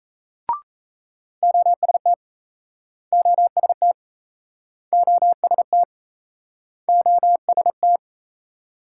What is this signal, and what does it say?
Morse code, heard as a keyed tone, about 40 wpm, 700 Hz. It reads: OST OST OST OST